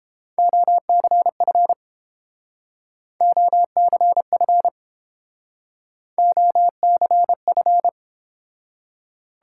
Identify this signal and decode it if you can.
Morse code, heard as a keyed tone, about 30 wpm, 700 Hz. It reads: OCF OCF OCF